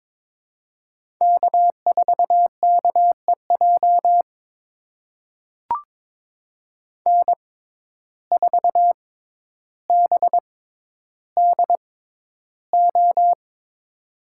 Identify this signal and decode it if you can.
Morse code, heard as a keyed tone, about 22 wpm, spaced as for 11 wpm, 700 Hz. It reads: K4KEJ N4BDO
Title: USA Call Signs - 5 Characters - 6x Character Spacing - 22wpm